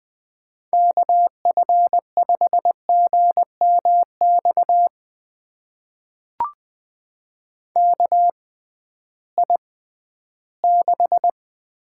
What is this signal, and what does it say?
Morse code, heard as a keyed tone, about 20 wpm, spaced as for 13 wpm, 700 Hz. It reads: KF5GMX KI6
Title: USA Call Signs - 6x Character Spacing - 20wpm